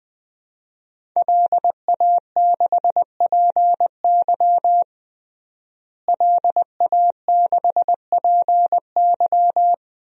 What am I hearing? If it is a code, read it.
Morse code, heard as a keyed tone, about 21 wpm, 700 Hz. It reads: LA6PY LA6PY
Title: DXCC-100 Call Signs - 5 Characters - Repeated 2x - 20wpm